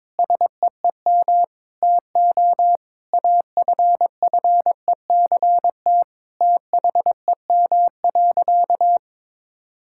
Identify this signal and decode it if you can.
Morse code, heard as a keyed tone, about 22 wpm, 700 Hz. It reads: SEEM TO AFFECT THEM.
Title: Sentences from Top 1000 Words - Review Part 6 - 22wpm